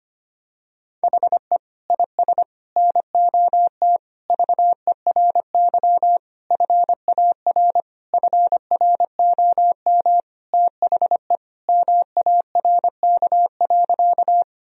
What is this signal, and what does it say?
Morse code, heard as a keyed tone, about 25 wpm, 700 Hz. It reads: HE IS NOT VERY FAR FROM THE MARK.
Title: Sentences from Top 1000 Words - Review Part 1 - 25wpm